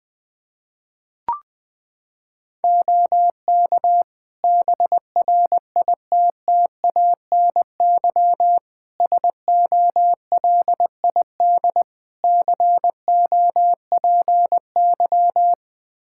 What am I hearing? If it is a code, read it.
Morse code, heard as a keyed tone, about 20 wpm, 700 Hz. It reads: OK BRITTANY SOLID COPY
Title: K1USN SST QSO Practice 20wpm